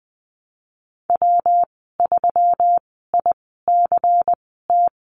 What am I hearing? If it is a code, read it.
Morse code, heard as a keyed tone, about 20 wpm, spaced as for 15 wpm, 700 Hz. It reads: W3ICT
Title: 154 Lesson 25g - Letter p - Call Signs 20wpm